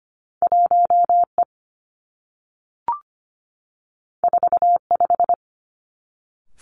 Morse code, heard as a keyed tone, about 25 wpm, 700 Hz. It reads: 1 E 4 5
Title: Two Challenging Letter-Number Combinations 25wpm